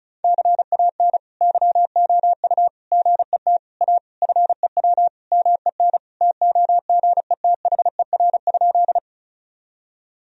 Morse code, heard as a keyed tone, about 34 wpm, 700 Hz. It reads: CAN YOU GET A FEW MEN TOGETHER?
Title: Sentences from Top 1000 Words - Review Part 1 - 35wpm